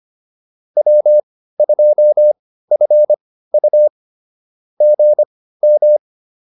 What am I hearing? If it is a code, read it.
Morse code, heard as a keyed tone, about 25 wpm, spaced as for 15 wpm, 600 Hz. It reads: W2FU GM